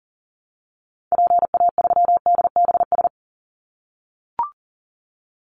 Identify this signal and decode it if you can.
Morse code, heard as a keyed tone, about 40 wpm, 700 Hz. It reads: PA3DBS
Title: Call Signs - 40wpm